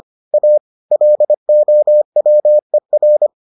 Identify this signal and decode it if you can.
Morse code, heard as a keyed tone, about 25 wpm, 600 Hz. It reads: A LOWER